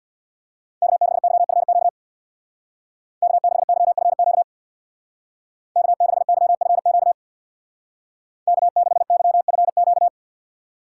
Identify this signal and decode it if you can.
Morse code, heard as a keyed tone, about 61 wpm, 700 Hz. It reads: K6YFQ K6YFQ K6YFQ K6YFQ
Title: Speed Racing - USA 1x3 Call Signs 50wpm